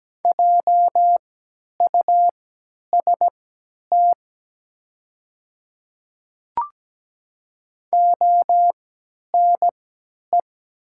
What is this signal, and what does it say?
Morse code, heard as a keyed tone, about 17 wpm, spaced as for 10 wpm, 700 Hz. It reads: JUST ONE
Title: 3x Character Spacing - Top 300 Words - 17wpm